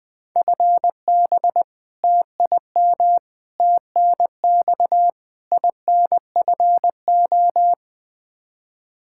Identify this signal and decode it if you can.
Morse code, heard as a keyed tone, about 20 wpm, 700 Hz. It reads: FB TIM TNX INFO
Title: Practice QSO - Exchange 3 - More Info - 20wpm